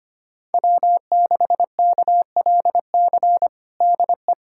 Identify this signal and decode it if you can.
Morse code, heard as a keyed tone, about 25 wpm, 700 Hz. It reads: W6KLC DE
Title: USA Callsign DE Callsign - 5 Characters - 25wpm